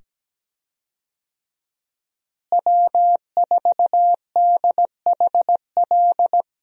Morse code, heard as a keyed tone, about 17 wpm, 700 Hz. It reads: W4DHL
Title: USA Call Signs - 5 Characters - Repeated 2x - 17wpm